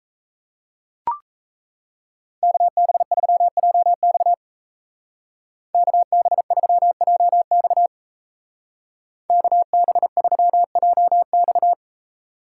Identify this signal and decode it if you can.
Morse code, heard as a keyed tone, about 37 wpm, 700 Hz. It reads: KB3JX KB3JX KB3JX